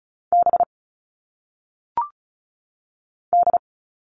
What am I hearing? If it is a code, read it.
Morse code, heard as a keyed tone, about 36 wpm, 700 Hz. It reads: B D
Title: All Cut Numbers - 1-Digit 35wpm